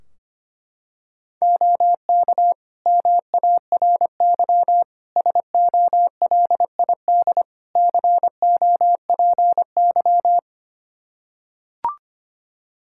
Morse code, heard as a keyed tone, about 25 wpm, 700 Hz. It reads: OK MARY SOLID COPY
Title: Practice QSO - Exchange 3 - More Info - 25wpm